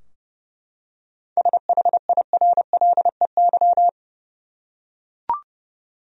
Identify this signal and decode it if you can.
Morse code, heard as a keyed tone, about 32 wpm, 700 Hz. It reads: SHIRLEY
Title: USA Common Names - 30wpm